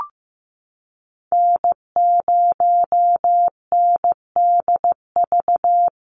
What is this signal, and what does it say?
Morse code, heard as a keyed tone, about 15 wpm, 700 Hz. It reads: N0NDV
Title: USA Call Signs 15wpm